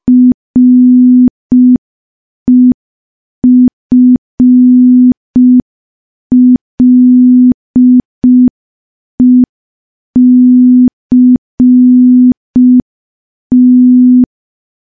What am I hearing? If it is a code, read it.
Morse code, heard as a keyed tone, about 5 wpm, 250 Hz. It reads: REFLECT
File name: morse / morse-training-05WPM